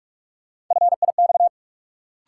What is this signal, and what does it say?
Morse code, heard as a keyed tone, about 43 wpm, 700 Hz. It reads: FIX
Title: Sentences from Top 1400 Words - New Word Drill - 45wpm